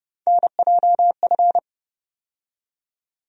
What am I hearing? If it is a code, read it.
Morse code, heard as a keyed tone, about 31 wpm, 700 Hz. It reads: NJF
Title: USA Call Sign Postfixes - 3 Letters 30wpm